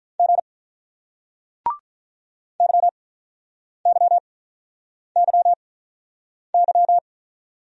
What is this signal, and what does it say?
Morse code, heard as a keyed tone, about 48 wpm, spaced as for 9 wpm, 700 Hz. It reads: G YYYY